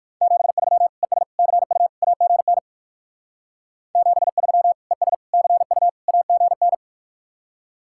Morse code, heard as a keyed tone, about 54 wpm, 700 Hz. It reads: 73 ES CU AGN 73 ES CU AGN
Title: Speed-Racing Common QSO Phrases - 45wpm